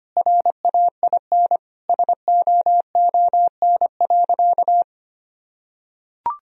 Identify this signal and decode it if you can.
Morse code, heard as a keyed tone, about 25 wpm, 700 Hz. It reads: RAIN SOON.
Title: Sentences from Top 1000 Words - Review Part 1 - 25wpm